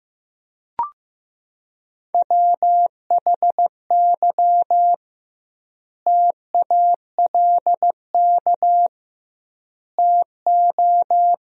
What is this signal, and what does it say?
Morse code, heard as a keyed tone, about 14 wpm, 700 Hz. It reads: WHY TALK TO
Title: 2x Word-Spacing - Sentences from Top 400 Words - 15wpm